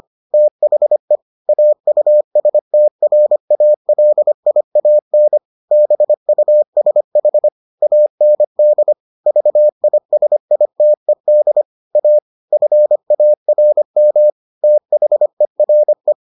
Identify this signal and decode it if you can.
Morse code, heard as a keyed tone, about 25 wpm, 600 Hz. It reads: THE AUSTRALIAN BUSH AND VISITED A FARM THERE